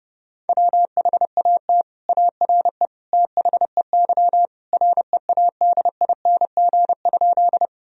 Morse code, heard as a keyed tone, about 32 wpm, 700 Hz. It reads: WHAT ARE THEY READING?
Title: Sentences from Top 1000 Words - Review Part 3 - 30wpm